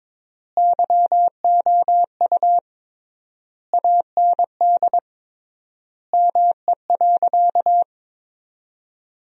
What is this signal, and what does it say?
Morse code, heard as a keyed tone, about 22 wpm, 700 Hz. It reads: YOU AND ME.
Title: 3x Word-Spacing - Sentences from Top 500 Words - 22wpm